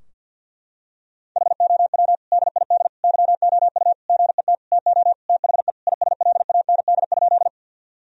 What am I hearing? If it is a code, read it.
Morse code, heard as a keyed tone, about 52 wpm, 700 Hz. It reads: HOW DID YOU GET TO THE ISLAND?
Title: Sentences from Top 1000 Words - Review Part 3 - 50wpm